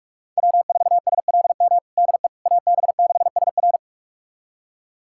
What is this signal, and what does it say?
Morse code, heard as a keyed tone, about 46 wpm, 700 Hz. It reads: W4SLM DE AB6SR
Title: USA Callsign DE Callsign - 5 Characters - 45wpm